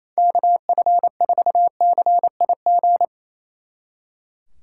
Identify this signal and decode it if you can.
Morse code, heard as a keyed tone, about 33 wpm, 700 Hz. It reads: KF4CIG
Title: Speed Racing - USA 2x3 Call Signs 28wpm